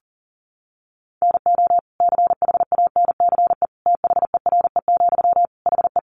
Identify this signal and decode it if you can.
Morse code, heard as a keyed tone, about 40 wpm, 700 Hz. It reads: NO CHANCE THERE, HE
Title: Sentences from Top 800 Words 40wpm